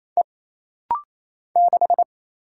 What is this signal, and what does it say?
Morse code, heard as a keyed tone, about 28 wpm, 700 Hz. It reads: E 6
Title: Single Letter-Number - Rapid-Fire 28wpm